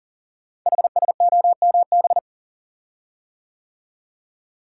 Morse code, heard as a keyed tone, about 39 wpm, 690 Hz. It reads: HSOMB
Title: Code Groups - 5 Letters - 40wpm